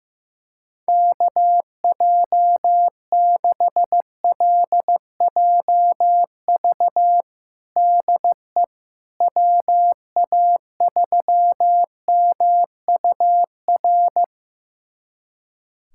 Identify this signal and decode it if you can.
Morse code, heard as a keyed tone, about 15 wpm, 700 Hz. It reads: KJ6LJV DE WA3MUR
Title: USA Callsign DE Callsign - 6 Characters - 15wpm